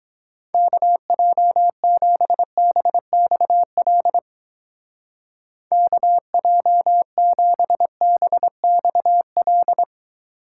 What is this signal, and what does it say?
Morse code, heard as a keyed tone, about 21 wpm, 700 Hz. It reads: KJ7BXL KJ7BXL
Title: Speed Racing - USA 2x3 Call Signs 17wpm